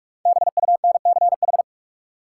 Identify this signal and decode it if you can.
Morse code, heard as a keyed tone, about 46 wpm, 700 Hz. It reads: BUNCH
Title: Five Letter Words - 45wpm